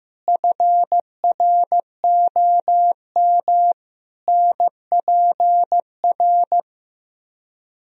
Morse code, heard as a keyed tone, about 15 wpm, 700 Hz. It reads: FROM NPR
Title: Speed-Racing - Sets of 5 Words 15wpm